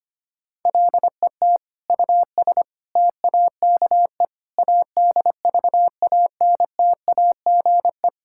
Morse code, heard as a keyed tone, about 25 wpm, 700 Hz. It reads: LET US TAKE ADVANTAGE
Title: Sentences from Top 1000 Words - Review Part 6 - 25wpm